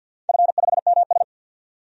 Morse code, heard as a keyed tone, about 50 wpm, 700 Hz. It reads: F5MS
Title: DXCC-100 Call Signs - 4 Characters - 50wpm